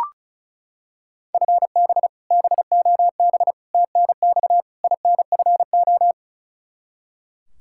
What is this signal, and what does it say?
Morse code, heard as a keyed tone, about 36 wpm, 700 Hz. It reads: FB BOB TNX INFO
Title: Practice QSO - Exchange 3 - More Info - 35wpm